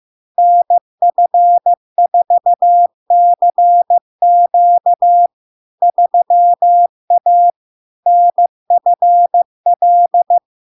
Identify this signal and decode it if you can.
Morse code, heard as a keyed tone, about 15 wpm, 700 Hz. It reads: NF4CQ 3A NFL
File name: LARK-FD-2022-Log-in-CW-15wpm-700-Hz